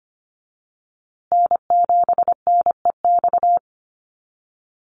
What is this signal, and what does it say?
Morse code, heard as a keyed tone, about 25 wpm, 700 Hz. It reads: N7NEX
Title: USA Call Signs Repeated 2x 25wpm